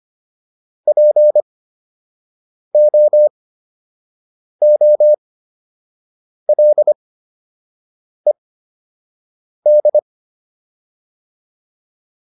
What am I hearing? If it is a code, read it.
Morse code, heard as a keyed tone, about 25 wpm, spaced as for 6 wpm, 600 Hz. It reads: POOLED